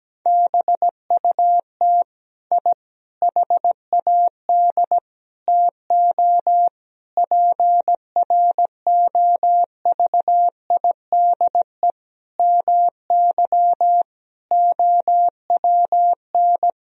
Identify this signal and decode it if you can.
Morse code, heard as a keyed tone, about 17 wpm, 700 Hz. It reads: BUT I HAD TO PROVIDE MY OWN
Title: Sentences from Top 1000 Words - Review Part 3 - 17wpm